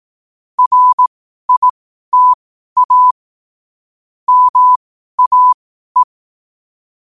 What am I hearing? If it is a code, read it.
Morse code, heard as a keyed tone, about 18 wpm, spaced as for 12 wpm, 1000 Hz. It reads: RITA MAE
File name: qotdfiles / 12wpm